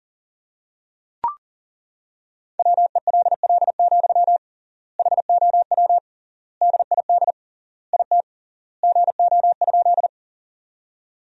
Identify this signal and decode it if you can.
Morse code, heard as a keyed tone, about 40 wpm, 700 Hz. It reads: WELL, HOW DID IT GO?
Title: 3x Word-Spacing - Sentences from Top 200 Words - 40wpm